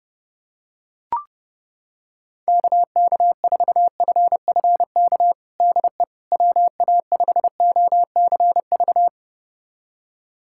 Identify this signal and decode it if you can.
Morse code, heard as a keyed tone, about 30 wpm, 700 Hz. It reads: KK4FFK DE WA5OCV
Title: USA Callsign DE Callsign - 6 Characters - 30wpm